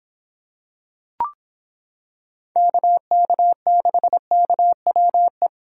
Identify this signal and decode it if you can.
Morse code, heard as a keyed tone, about 26 wpm, 700 Hz. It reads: KK6KWE